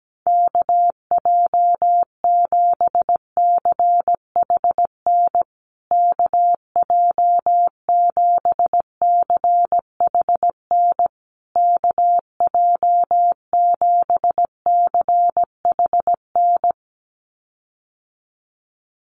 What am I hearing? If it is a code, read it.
Morse code, heard as a keyed tone, about 17 wpm, 700 Hz. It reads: KJ7CHN KJ7CHN KJ7CHN
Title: USA Call Signs Repeated 3x 17wpm